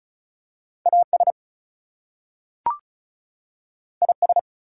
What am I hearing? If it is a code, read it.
Morse code, heard as a keyed tone, about 36 wpm, 700 Hz. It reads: AS IS
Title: Two Letter Words - 35wpm